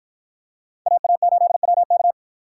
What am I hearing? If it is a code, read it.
Morse code, heard as a keyed tone, about 49 wpm, 700 Hz. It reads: AA8WK